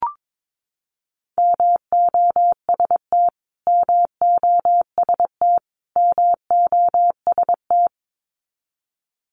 Morse code, heard as a keyed tone, about 22 wpm, 700 Hz. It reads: MOST MOST MOST
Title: Top 200 Words Repeated 22wpm